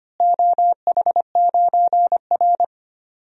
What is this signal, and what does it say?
Morse code, heard as a keyed tone, about 25 wpm, 700 Hz. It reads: OH9R
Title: DXCC-100 Call Signs - 4 Characters - Repeated 3x - 25wpm